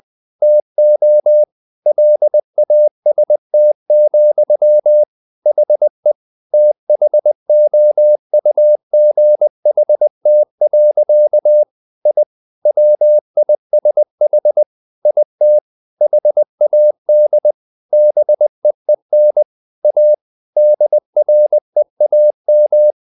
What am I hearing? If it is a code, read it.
Morse code, heard as a keyed tone, about 20 wpm, 600 Hz. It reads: TO LAST, HE THOUGHT. I WISH IT HAD BEEN A DREAM
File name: the old man and the sea part 2 20wpm